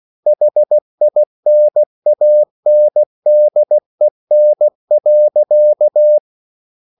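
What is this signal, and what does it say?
Morse code, heard as a keyed tone, about 16 wpm, 600 Hz. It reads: HINANDEN.
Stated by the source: Slow (QRS) Morse code at 16 wpm, from CW med Gnister.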